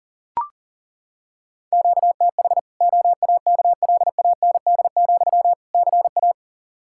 Morse code, heard as a keyed tone, about 40 wpm, 700 Hz. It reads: QTH OAKLAND, CA
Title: K1USN SST QSO Practice 40wpm